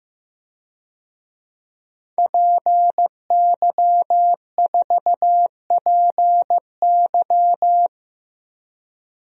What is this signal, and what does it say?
Morse code, heard as a keyed tone, about 15 wpm, 700 Hz. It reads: PY4PY
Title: DXCC-100 Call Signs - 5 Characters - 15wpm